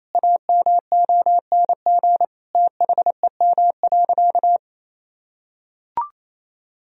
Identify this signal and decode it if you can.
Morse code, heard as a keyed tone, about 28 wpm, 700 Hz. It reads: AMONG THEM.
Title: Sentences from Top 1000 Words - Review Part 2 - 28wpm